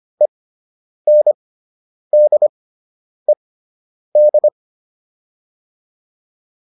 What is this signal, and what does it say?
Morse code, heard as a keyed tone, about 25 wpm, spaced as for 9 wpm, 600 Hz. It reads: ENDED
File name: ED5 09